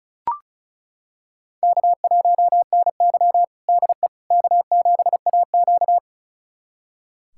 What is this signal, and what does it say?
Morse code, heard as a keyed tone, about 36 wpm, 700 Hz. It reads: K1NY DE K7AQ
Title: USA Callsign DE Callsign - 4 Characters - 35wpm